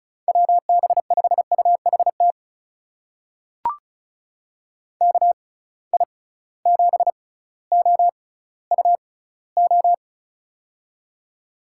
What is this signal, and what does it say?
Morse code, heard as a keyed tone, about 36 wpm, spaced as for 18 wpm, 700 Hz. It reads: WB5UHT KI7OUO
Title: USA Call Signs - 6x Character Spacing - 35wpm